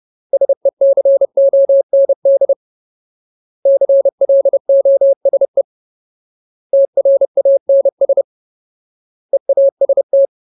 Morse code, heard as a keyed tone, about 30 wpm, 550 Hz. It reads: SECOND CLOSE TRANS EAST
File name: Ultimate Word List / 30WPM Ultimate